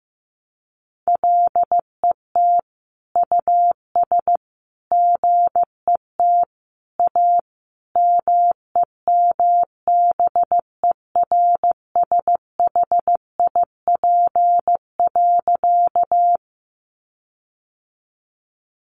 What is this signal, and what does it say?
Morse code, heard as a keyed tone, about 15 wpm, 700 Hz. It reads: LET US GET A MEMBERSHIP.
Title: Sentences from Top 1400 Words 15wpm